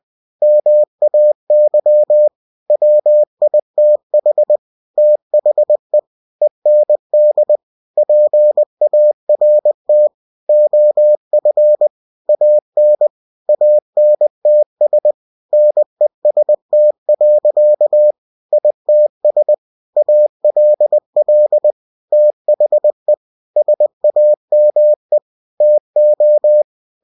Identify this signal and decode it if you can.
Morse code, heard as a keyed tone, about 20 wpm, 600 Hz. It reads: MAY WITH THE END PART OF AN ANTS NEST. ITS ALL THE SAME TO
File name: cw-books / winnie-the-pooh / winnie-20wpm-page-0080